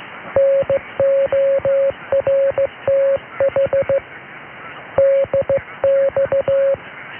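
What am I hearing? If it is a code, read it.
Morse code, heard as a keyed tone, about 16 wpm, 550 Hz. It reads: NORTH DX
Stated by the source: Morse heard during a Radio North DX test on 846 kHz mediumwave, logged in transatlantic DX logs from SW Scotland.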